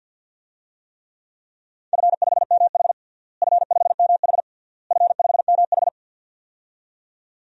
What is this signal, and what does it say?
Morse code, heard as a keyed tone, about 52 wpm, 700 Hz. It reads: F5MH F5MH F5MH
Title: DXCC-100 Call Signs - 4 Characters - Repeated 3x - 50wpm